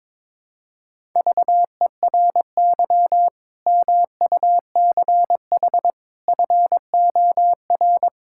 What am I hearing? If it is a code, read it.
Morse code, heard as a keyed tone, about 22 wpm, 700 Hz. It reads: VERY MUCH FOR
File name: practice / 22wpm / Build A Phrase - 5-Word Phrases - 22wpm